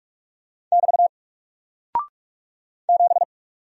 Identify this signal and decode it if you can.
Morse code, heard as a keyed tone, about 47 wpm, 700 Hz. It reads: <BT> 7
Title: Single Character (Letter:Number:Punctuation:Prosign) Alt 45wpm